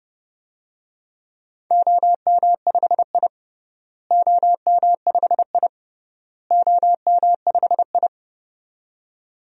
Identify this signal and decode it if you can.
Morse code, heard as a keyed tone, about 32 wpm, 700 Hz. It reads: OM5I OM5I OM5I
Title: DXCC-100 Call Signs - 4 Characters - Repeated 3x - 30wpm